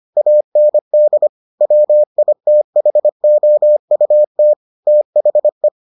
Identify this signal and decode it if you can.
Morse code, heard as a keyed tone, about 25 wpm, 600 Hz. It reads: AND WITHOUT THE